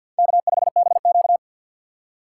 Morse code, heard as a keyed tone, about 62 wpm, 700 Hz. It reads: K5BQ
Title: Speed Racing - USA Call Signs 50wpm